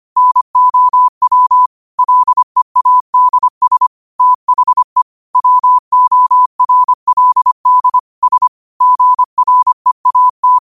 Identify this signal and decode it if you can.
Morse code, heard as a keyed tone, about 25 wpm, 1000 Hz. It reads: NOW LEADS THE WORLD S GREAT